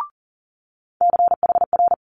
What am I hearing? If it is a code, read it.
Morse code, heard as a keyed tone, about 42 wpm, 700 Hz. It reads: CHR
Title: Multi-letter Phonemes Quick Fire 40wpm